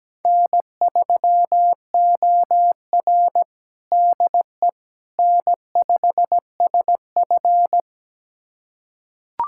Morse code, heard as a keyed tone, about 17 wpm, 700 Hz. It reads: N3OR DE N5SF
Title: USA Callsign DE Callsign - 4 Characters - 17wpm